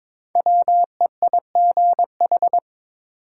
Morse code, heard as a keyed tone, about 22 wpm, 700 Hz. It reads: WEIGH
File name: 5 Letter Words - First 3 Letters Given - 22wpm